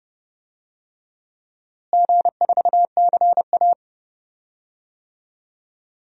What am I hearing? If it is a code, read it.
Morse code, heard as a keyed tone, about 30 wpm, 700 Hz. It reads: G4CA
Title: DXCC-100 Call Signs - 4 Characters - Repeated 2x - 30wpm